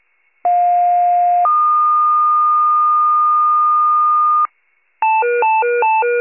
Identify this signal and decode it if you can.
Two-tone page, heard as a one-second tone, then a three-second tone, about 710 then 1220 Hz. It is heard from 0.4 to 4.5 s.
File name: pager / 46 RESCUE / wcalert